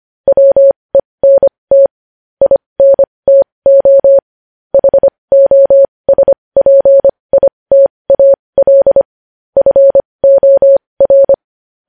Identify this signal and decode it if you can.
Morse code, heard as a keyed tone, about 25 wpm, spaced as for 20 wpm, 550 Hz. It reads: WENT INTO HOSPITAL FOR
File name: podcast / News headlines-1x-25x20